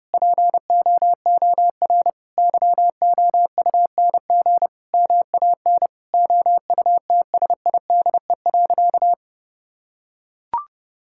Morse code, heard as a keyed tone, about 30 wpm, 700 Hz. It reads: POOR YOUNG MAN OUTSIDE.
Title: Sentences from Top 1000 Words - Review Part 6 - 30wpm